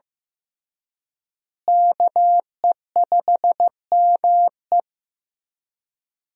Morse code, heard as a keyed tone, about 15 wpm, 700 Hz. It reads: KE5ME
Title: USA Call Signs - 5 Characters - Repeated 3x - 15wpm